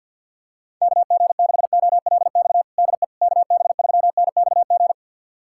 Morse code, heard as a keyed tone, about 51 wpm, 700 Hz. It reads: KG6OLX DE KB3NXG
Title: USA Callsign DE Callsign - 6 Characters - 50wpm